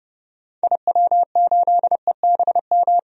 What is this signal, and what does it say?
Morse code, heard as a keyed tone, about 32 wpm, 700 Hz. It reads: IW8EBM
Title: DXCC-100 Call Signs - 6 Characters - 30wpm